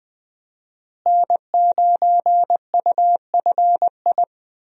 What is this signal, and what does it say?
Morse code, heard as a keyed tone, about 20 wpm, 700 Hz. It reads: N9UFI
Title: USA Call Signs - 5 Characters - Repeated 3x - 20wpm